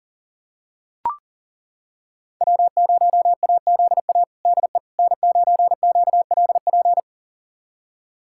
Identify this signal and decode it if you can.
Morse code, heard as a keyed tone, about 40 wpm, 700 Hz. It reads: W0AZA DE N9QLP
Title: USA Callsign DE Callsign - 5 Characters - 40wpm